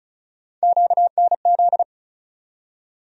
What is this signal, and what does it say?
Morse code, heard as a keyed tone, about 36 wpm, 700 Hz. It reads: QNZ
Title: USA Call Sign Postfixes - 3 Letters 35wpm